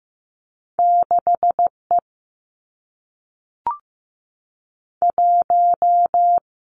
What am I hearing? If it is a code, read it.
Morse code, heard as a keyed tone, about 15 wpm, 700 Hz. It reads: 6 E 1
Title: Two Challenging Letter-Number Combinations 15wpm